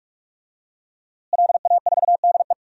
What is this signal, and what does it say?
Morse code, heard as a keyed tone, about 45 wpm, 700 Hz. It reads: LA4DE
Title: DXCC-100 Call Signs - 5 Characters - Repeated 2x - 45wpm